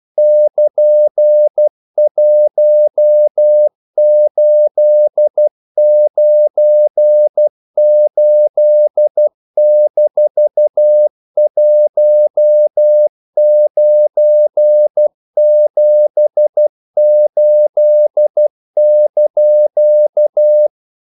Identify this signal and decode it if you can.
Morse code, heard as a keyed tone, about 12 wpm, 600 Hz. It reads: <KN>1898-1978)
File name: CW / Practice / EnglishQuotations0112